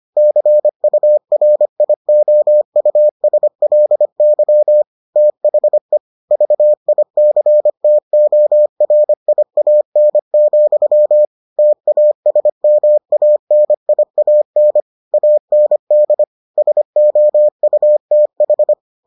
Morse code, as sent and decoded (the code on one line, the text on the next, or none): -.-. ..- .-. .. --- ..- ... .-.. -.-- / - .... / ...- .. -.-. - --- .-. .. .- -. --..-- / - .- ... -- .- -. .. .- -. / .- -. -.. / ... --- ..- - ....
CURIOUSLY THE VICTORIAN, TASMANIAN AND SOUTH